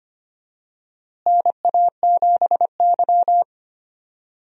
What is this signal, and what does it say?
Morse code, heard as a keyed tone, about 25 wpm, 700 Hz. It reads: NA7Y
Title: USA Call Signs - 4 Characters - Repeated 3x - 25wpm